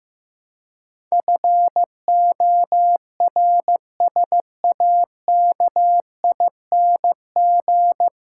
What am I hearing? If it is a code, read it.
Morse code, heard as a keyed tone, about 15 wpm, 700 Hz. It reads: FORSAKING
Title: Pangram Challenge v2 - 15wpm